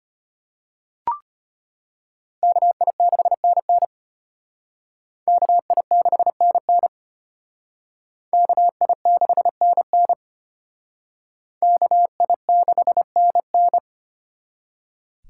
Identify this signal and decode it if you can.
Morse code, heard as a keyed tone, about 34 wpm, 700 Hz. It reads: KI6NN KI6NN KI6NN KI6NN